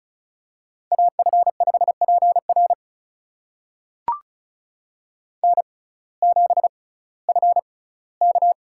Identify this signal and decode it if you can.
Morse code, heard as a keyed tone, about 36 wpm, spaced as for 19 wpm, 700 Hz. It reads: AF5PR N7FK
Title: USA Call Signs - 5 Characters - 6x Character Spacing - 35wpm